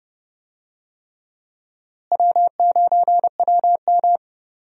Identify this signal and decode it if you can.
Morse code, heard as a keyed tone, about 29 wpm, 700 Hz. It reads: W9WM
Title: USA Call Signs - 4 Characters - Repeated 3x - 30wpm